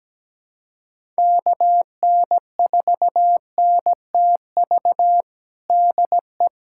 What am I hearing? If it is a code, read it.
Morse code, heard as a keyed tone, about 17 wpm, 700 Hz. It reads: KN4NTV DE
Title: USA Callsign DE Callsign 17wpm